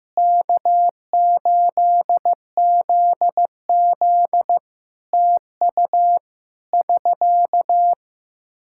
Morse code, heard as a keyed tone, about 16 wpm, 700 Hz. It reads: K8ZZ TU <SK>
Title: K1USN SST QSO Practice 15wpm